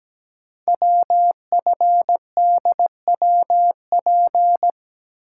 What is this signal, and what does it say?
Morse code, heard as a keyed tone, about 17 wpm, 700 Hz. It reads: WFDWP
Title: Code Groups - 5 Letters - 17wpm